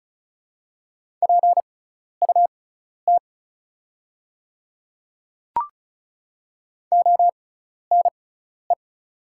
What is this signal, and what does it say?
Morse code, heard as a keyed tone, about 34 wpm, spaced as for 12 wpm, 700 Hz. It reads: PUT ONE
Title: 6x Character Spacing - Top 200 Words - 35wpm